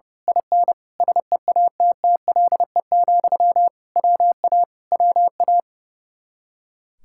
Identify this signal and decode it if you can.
Morse code, heard as a keyed tone, about 28 wpm, 700 Hz. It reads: IN SEATTLE, WA WA
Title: Speed-Racing Common QSO Phrases Encore - 30wpm